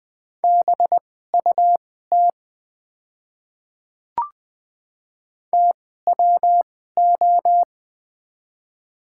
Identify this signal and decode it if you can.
Morse code, heard as a keyed tone, about 21 wpm, spaced as for 15 wpm, 700 Hz. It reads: BUT TWO
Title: 2x Character Spacing - Top 200 Words - 20wpm